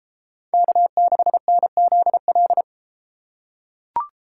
Morse code, heard as a keyed tone, about 32 wpm, 700 Hz. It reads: K6NZL